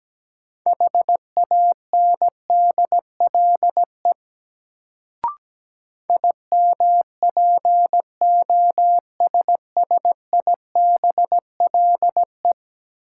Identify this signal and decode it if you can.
Morse code, heard as a keyed tone, about 17 wpm, 700 Hz. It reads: HANDLE IMPOSSIBLE
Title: Sentences from Top 1100 Words - New Word Drill - 17wpm